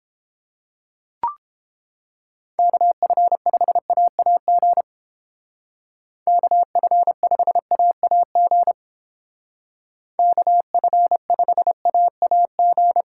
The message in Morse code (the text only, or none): KF5AAG KF5AAG KF5AAG